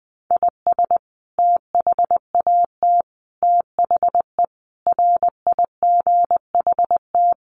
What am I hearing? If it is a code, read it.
Morse code, heard as a keyed tone, about 20 wpm, 700 Hz. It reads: IS THAT THE RIGHT
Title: Sentences from Top 1200 Words 20wpm